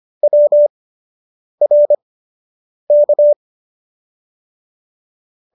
Morse code, heard as a keyed tone, about 25 wpm, spaced as for 8 wpm, 590 Hz. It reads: WRK